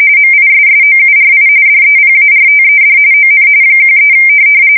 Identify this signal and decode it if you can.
RTTY, about 45.45 baud. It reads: MISSOURI DEPARTED ON 13
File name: fsk170 / USS-Missouri